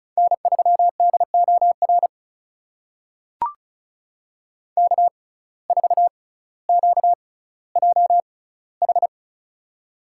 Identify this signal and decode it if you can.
Morse code, heard as a keyed tone, about 36 wpm, spaced as for 18 wpm, 700 Hz. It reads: N3DOR K4QJH
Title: USA Call Signs - 5 Characters - 6x Character Spacing - 35wpm